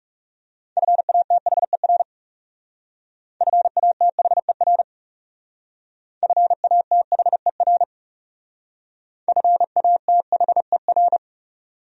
Morse code, heard as a keyed tone, about 37 wpm, 700 Hz. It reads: FATHER FATHER FATHER FATHER
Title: Speed-Racing Top 300 Words 30wpm